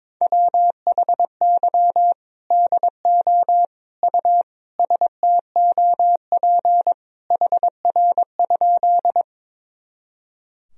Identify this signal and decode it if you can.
Morse code, heard as a keyed tone, about 22 wpm, 700 Hz. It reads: WHY DO U STOP HR?
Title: Sentences from Top 300 Words and Abbreviations 22wpm